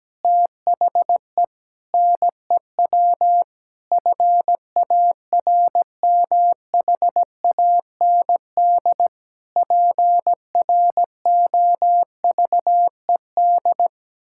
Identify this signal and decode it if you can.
Morse code, heard as a keyed tone, about 17 wpm, 700 Hz. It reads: THE NEW FARMHAND PROVED
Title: Pangram Challenge v2 - 17wpm